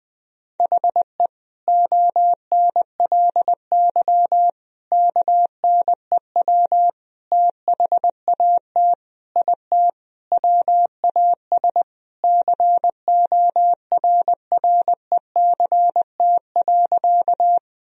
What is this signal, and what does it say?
Morse code, heard as a keyed tone, about 20 wpm, 700 Hz. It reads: HE ONLY KNEW THAT IT WAS CORRECT.
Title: Sentences from Top 1000 Words - Review Part 2 - 20wpm